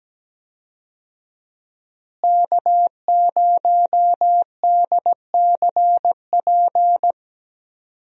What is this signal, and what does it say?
Morse code, heard as a keyed tone, about 17 wpm, 700 Hz. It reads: K0DCP
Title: USA Call Signs - 5 Characters - Repeated 2x - 17wpm